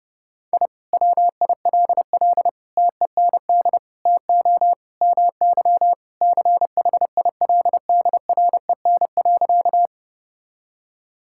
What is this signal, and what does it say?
Morse code, heard as a keyed tone, about 30 wpm, 700 Hz. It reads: I WILL TEND TO MY CHILDREN.
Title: Sentences from Top 1000 Words - Review Part 5 - 30wpm